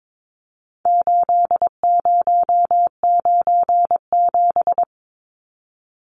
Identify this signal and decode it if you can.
Morse code, heard as a keyed tone, about 22 wpm, 700 Hz. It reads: 8097
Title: Numbers - 4-Digit 22wpm